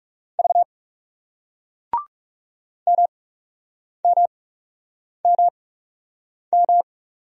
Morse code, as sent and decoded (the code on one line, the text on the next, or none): ...- / -- -- -- --
V MMMM